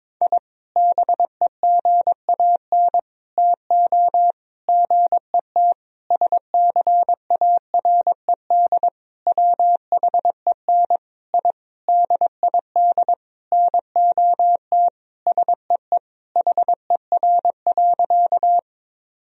Morse code, heard as a keyed tone, about 22 wpm, 700 Hz. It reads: I BEGAN TO GET SCARED WHEN I DID NOT SEE HER.
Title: Sentences from Top 1000 Words - Review Part 4 - 22wpm